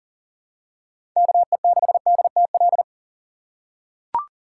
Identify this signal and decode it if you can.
Morse code, heard as a keyed tone, about 40 wpm, 700 Hz. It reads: KE6DTL
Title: USA Call Signs - 2x Character Spacing - 40wpm